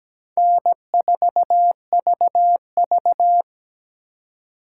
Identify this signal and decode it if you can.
Morse code, heard as a keyed tone, about 17 wpm, 700 Hz. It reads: N4VV
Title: USA Call Signs - 4 Characters - 2x Character Spacing - 17wpm